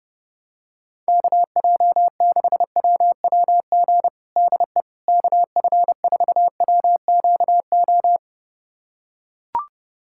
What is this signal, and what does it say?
Morse code, heard as a keyed tone, about 30 wpm, 700 Hz. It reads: KJ6WWG DE KF4WQO
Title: USA Callsign DE Callsign - 6 Characters - 30wpm